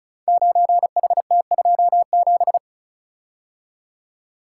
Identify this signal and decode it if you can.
Morse code, heard as a keyed tone, about 36 wpm, 690 Hz. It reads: 9HT27
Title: Code Groups - 5 Letter:Numbers - 35wpm